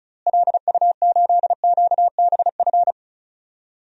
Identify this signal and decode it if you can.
Morse code, heard as a keyed tone, about 34 wpm, 700 Hz. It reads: LU8QBF
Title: DXCC-100 Call Signs - 6 Characters - Repeated 2x - 35wpm